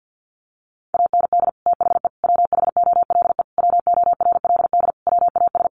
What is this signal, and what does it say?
Morse code, heard as a keyed tone, about 50 wpm, 700 Hz. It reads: AND THE WHOLE WORLD WAS
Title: Sentences from Top 1300 Words 50wpm